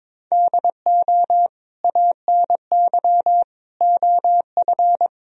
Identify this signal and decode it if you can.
Morse code, heard as a keyed tone, about 22 wpm, 700 Hz. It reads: DO ANY OF
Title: Sentences from Top 1000 Words - Review Part 3 - 22wpm